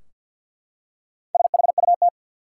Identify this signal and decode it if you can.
Morse code, heard as a keyed tone, about 50 wpm, 700 Hz. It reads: SHUT
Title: Four Letter Words - 50wpm